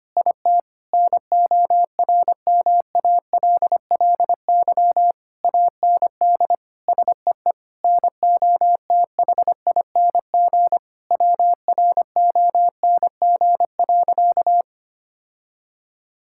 Morse code, heard as a keyed tone, about 25 wpm, 700 Hz. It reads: IT NORMALLY AND SEE NOTHING WRONG.
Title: Sentences from Top 1000 Words - Review Part 6 - 25wpm